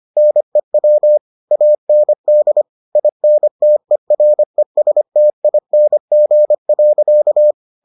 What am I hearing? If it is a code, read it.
Morse code, heard as a keyed tone, about 25 wpm, 600 Hz. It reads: NEW AND INTERESTING.